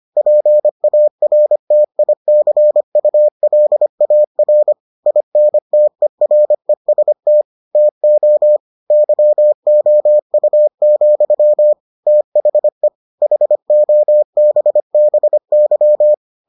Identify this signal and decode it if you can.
Morse code, heard as a keyed tone, about 25 wpm, 600 Hz. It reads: PARTICULAR INTEREST TO YOU, THE HOBBY